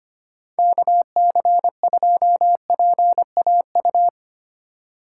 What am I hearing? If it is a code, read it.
Morse code, heard as a keyed tone, about 25 wpm, 700 Hz. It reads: KC2PAU
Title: USA Call Signs - 6 Characters - 2x Character Spacing - 25wpm